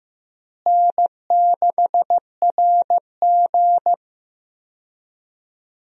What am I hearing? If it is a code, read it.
Morse code, heard as a keyed tone, about 15 wpm, 700 Hz. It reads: N6RG
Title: USA Call Signs - 4 Characters - Repeated 2x - 15wpm